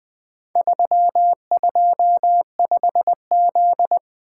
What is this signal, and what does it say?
Morse code, heard as a keyed tone, about 20 wpm, 700 Hz. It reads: 325Z
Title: Four Letter-Number Combinations - 20wpm